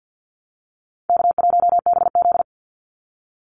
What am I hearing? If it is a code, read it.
Morse code, heard as a keyed tone, about 49 wpm, 700 Hz. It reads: K1BZ
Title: Call Signs Repeated 3x - 50wpm